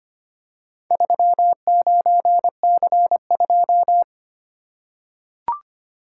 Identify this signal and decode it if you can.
Morse code, heard as a keyed tone, about 25 wpm, 700 Hz. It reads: 39C2
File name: practice / 25wpm / Four Letter-Number Combinations - 25wpm